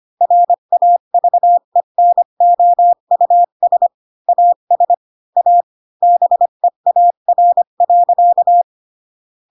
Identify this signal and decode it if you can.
Morse code, heard as a keyed tone, about 25 wpm, 700 Hz. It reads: RAVENOUS AS A BEAR.
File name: Pinocchio / Pinocchio ch5 20 25 ik1hge